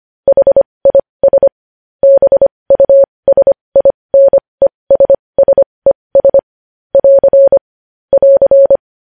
Morse code, heard as a keyed tone, about 25 wpm, spaced as for 20 wpm, 550 Hz. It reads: HIS BUSINESSES <AR> <AR>
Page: News headlines in Morse code at 20 wpm